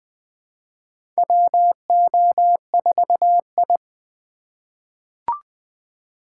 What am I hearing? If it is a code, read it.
Morse code, heard as a keyed tone, about 20 wpm, 700 Hz. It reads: WO4I
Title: USA Call Signs - 4 Characters - 2x Character Spacing - 20wpm